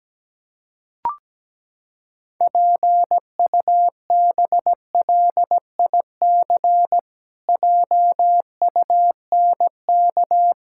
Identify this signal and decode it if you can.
Morse code, heard as a keyed tone, about 17 wpm, 700 Hz. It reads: PUBLIC JUNK
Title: Pangram Challenge v2 - 17wpm